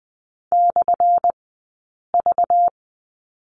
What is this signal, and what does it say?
Morse code, heard as a keyed tone, about 20 wpm, spaced as for 8 wpm, 700 Hz. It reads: /V
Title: 190 Lesson 31d - Letter v - Two Characters 20wpm